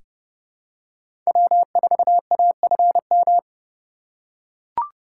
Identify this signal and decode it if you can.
Morse code, heard as a keyed tone, about 30 wpm, 700 Hz. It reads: W4AFM
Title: USA Call Signs - 5 Characters - 4x Character Spacing - 30wpm